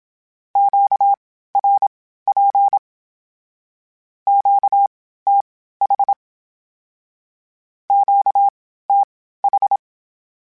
Morse code, heard as a keyed tone, about 27 wpm, spaced as for 15 wpm, 800 Hz. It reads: QRP QTH QTH
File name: (spaced) / Q-Calls Spaced-mp3s / Q Calls Twice sp @25